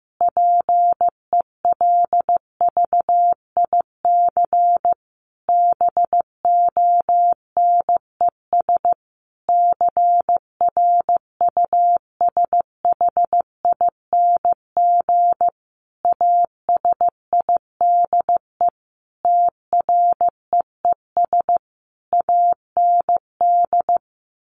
Morse code, heard as a keyed tone, about 15 wpm, 700 Hz. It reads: PELVIC BONES CRUSHING ASIDE TREES AND
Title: A Sound Of Thunder 15wpm